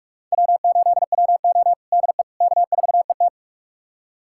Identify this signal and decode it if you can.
Morse code, heard as a keyed tone, about 41 wpm, 700 Hz. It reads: W8WO DE K4ET
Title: USA Callsign DE Callsign - 4 Characters - 45wpm